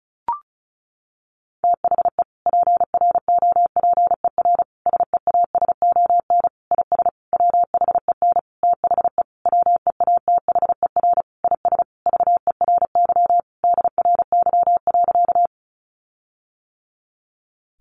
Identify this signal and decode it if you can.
Morse code, heard as a keyed tone, about 34 wpm, 700 Hz. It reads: THE PROPER SEASON IS WHEN THE WEATHER IS VERY DRY.
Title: Sentences from Top 1400 Words 35wpm